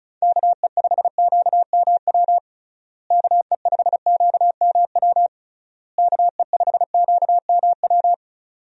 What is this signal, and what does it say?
Morse code, heard as a keyed tone, about 36 wpm, 700 Hz. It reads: KE5QMW KE5QMW KE5QMW
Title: USA Call Signs - 6 Characters - Repeated 3x - 35wpm